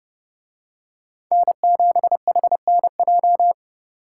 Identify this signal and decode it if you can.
Morse code, heard as a keyed tone, about 29 wpm, 700 Hz. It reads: N7HNJ